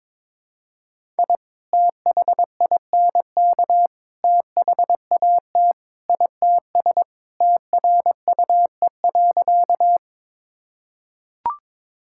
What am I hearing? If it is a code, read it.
Morse code, heard as a keyed tone, about 22 wpm, 700 Hz. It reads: I THINK THAT ITS TRUE.
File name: practice / 22wpm / Sentences with 2-Word Phrases - 22wpm